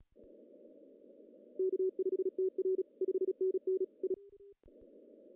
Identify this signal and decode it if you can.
Morse code, heard as a keyed tone, about 34 wpm, 410 Hz. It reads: K5TR 5NN I
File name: CW / cqww-cw-k5tr